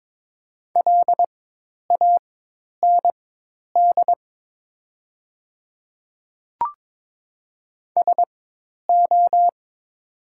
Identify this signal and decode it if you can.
Morse code, heard as a keyed tone, about 22 wpm, spaced as for 10 wpm, 700 Hz. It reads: LAND SO
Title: 4x Character Spacing - Top 200 Words - 22wpm